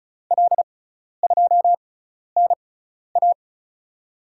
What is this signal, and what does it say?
Morse code, heard as a keyed tone, about 34 wpm, spaced as for 12 wpm, 700 Hz. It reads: L2NA
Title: USA Call Signs - 5 Characters - 6x Character Spacing - 35wpm